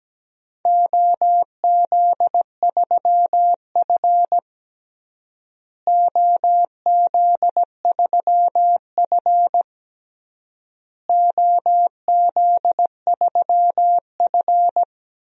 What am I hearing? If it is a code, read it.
Morse code, heard as a keyed tone, about 17 wpm, 700 Hz. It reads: OZ3F OZ3F OZ3F
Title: DXCC-100 Call Signs - 4 Characters - Repeated 3x - 17wpm